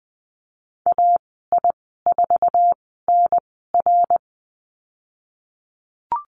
Morse code, heard as a keyed tone, about 20 wpm, spaced as for 15 wpm, 700 Hz. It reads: AI4NR
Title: 055 Lesson 10g - Letter r - Call Signs 20wpm